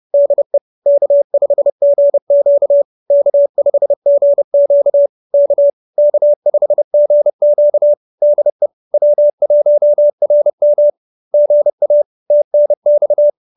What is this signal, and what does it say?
Morse code, heard as a keyed tone, about 30 wpm, 590 Hz. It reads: DE K5GQ K5GQ K K5GQ DE W1RM GA TNX
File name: QSO308 30